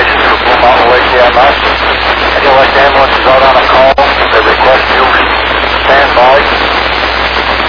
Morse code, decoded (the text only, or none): none